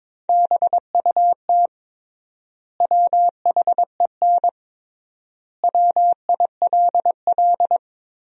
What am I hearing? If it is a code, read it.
Morse code, heard as a keyed tone, about 22 wpm, 700 Hz. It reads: BUT WHEN WILL
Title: 3x Word-Spacing - Sentences from Top 100 Words - 22wpm